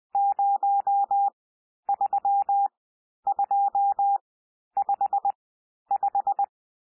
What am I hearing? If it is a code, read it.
Morse code, heard as a keyed tone, about 20 wpm, spaced as for 11 wpm, 800 Hz. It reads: 03255
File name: cwmp3 / N1 SOUND NUMBERS 20 wpm